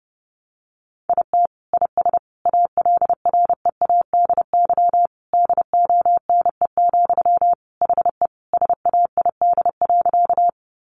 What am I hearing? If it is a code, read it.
Morse code, heard as a keyed tone, about 28 wpm, 700 Hz. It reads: IT IS ALREADY DONE, HE SAID.
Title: Sentences from Top 700 Words 30wpm